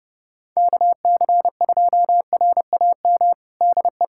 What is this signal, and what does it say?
Morse code, heard as a keyed tone, about 32 wpm, 700 Hz. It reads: KC2RAM DE
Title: USA Callsign DE Callsign - 6 Characters - 30wpm